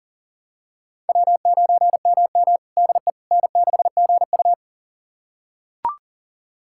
Morse code, heard as a keyed tone, about 40 wpm, 700 Hz. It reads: W9MM DE N6GU
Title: USA Callsign DE Callsign - 4 Characters - 40wpm